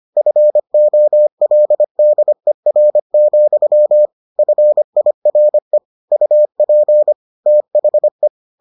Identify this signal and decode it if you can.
Morse code, heard as a keyed tone, about 25 wpm, 600 Hz. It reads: FOLDER, FIRE UP THE